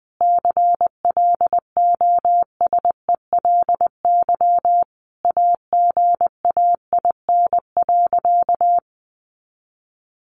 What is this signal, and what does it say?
Morse code, heard as a keyed tone, about 20 wpm, 700 Hz. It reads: CLOSELY AGAIN.
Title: Sentences from Top 1400 Words 20wpm